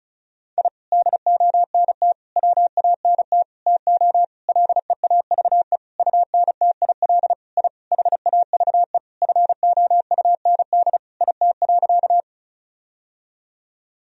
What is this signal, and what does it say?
Morse code, heard as a keyed tone, about 34 wpm, 700 Hz. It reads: I DONT WANT TO LEAVE UNTIL I HAVE FOUND IT.
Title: Sentences from Top 1000 Words - Review Part 1 - 35wpm